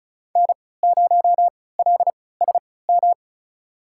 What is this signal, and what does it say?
Morse code, heard as a keyed tone, about 34 wpm, spaced as for 20 wpm, 690 Hz. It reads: N0LSM
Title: USA Call Signs - 5 Characters - 3x Character Spacing - 35wpm